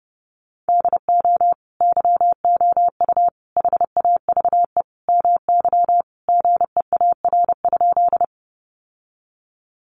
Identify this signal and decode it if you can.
Morse code, heard as a keyed tone, about 30 wpm, 700 Hz. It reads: DO YOU HAVE MY GEAR?
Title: Sentences from Top 1400 Words 30wpm